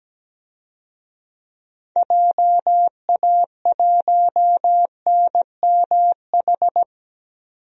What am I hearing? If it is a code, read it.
Morse code, heard as a keyed tone, about 17 wpm, 700 Hz. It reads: JA1NMH
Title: DXCC-100 Call Signs - 6 Characters - 17wpm